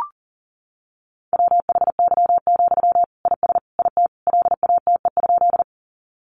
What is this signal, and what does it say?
Morse code, heard as a keyed tone, about 40 wpm, 700 Hz. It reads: WHY, IS IT LATE?
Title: Sentences from Top 300 Words 40wpm